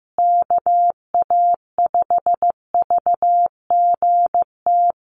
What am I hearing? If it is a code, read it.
Morse code, heard as a keyed tone, about 15 wpm, 700 Hz. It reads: KA5VGT
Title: USA Call Signs Repeated 3x 15wpm